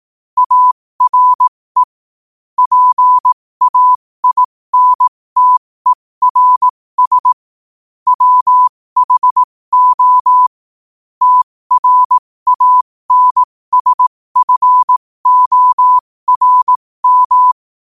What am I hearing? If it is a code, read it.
Morse code, heard as a keyed tone, about 18 wpm, spaced as for 15 wpm, 1000 Hz. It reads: ARE PAINTERS WHO TRANSFORM
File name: qotdfiles / 15wpm